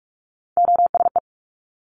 Morse code, heard as a keyed tone, about 46 wpm, 690 Hz. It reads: OSE
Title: Top Three Letter Combinations in Words 45wpm